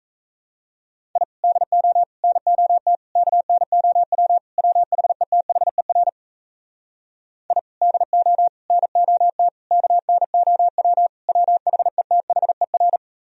Morse code, heard as a keyed tone, about 38 wpm, 700 Hz. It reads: I DO NOT KNOW WHETHER I DO NOT KNOW WHETHER